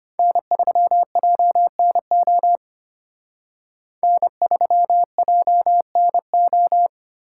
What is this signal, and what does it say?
Morse code, heard as a keyed tone, about 32 wpm, 700 Hz. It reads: N3JNO N3JNO